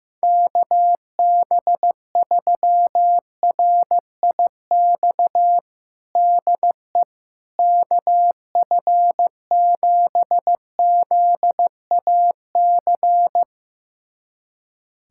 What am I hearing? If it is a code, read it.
Morse code, heard as a keyed tone, about 15 wpm, 700 Hz. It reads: KB3RIX DE KF7ZAC
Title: USA Callsign DE Callsign - 6 Characters - 15wpm